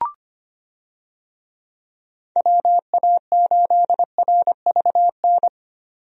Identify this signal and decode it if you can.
Morse code, heard as a keyed tone, about 25 wpm, 700 Hz. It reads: WA8RVN
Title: USA Call Signs - 6 Characters - 25wpm